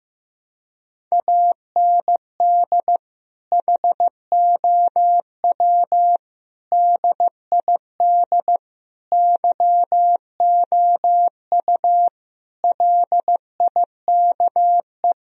Morse code, heard as a keyed tone, about 15 wpm, 700 Hz. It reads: AND HOW DID YOU LIKE
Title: Sentences from Top 1000 Words - Review Part 4 - 15wpm